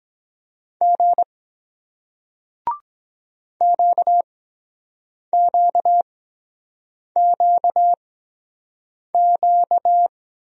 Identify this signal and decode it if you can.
Morse code, heard as a keyed tone, about 23 wpm, spaced as for 7 wpm, 700 Hz. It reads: G QQQQ